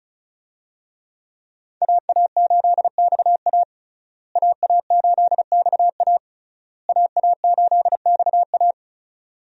Morse code, heard as a keyed tone, about 36 wpm, 700 Hz. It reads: AA8XA AA8XA AA8XA
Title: USA Call Signs - 5 Characters - Repeated 3x - 35wpm